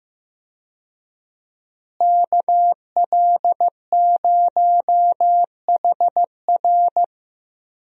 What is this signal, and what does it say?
Morse code, heard as a keyed tone, about 15 wpm, 700 Hz. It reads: KL0HR
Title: USA Call Signs - 5 Characters - Repeated 3x - 15wpm